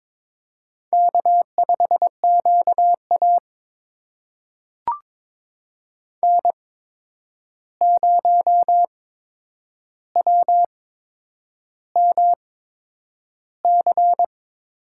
Morse code, heard as a keyed tone, about 22 wpm, spaced as for 9 wpm, 700 Hz. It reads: K5QA N0WMC
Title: USA Call Signs - 8x Character Spacing - 22wpm